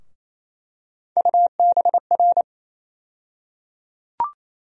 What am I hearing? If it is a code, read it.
Morse code, heard as a keyed tone, about 28 wpm, 700 Hz. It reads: UBR
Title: USA Call Sign Postfixes - 3 Letters 28wpm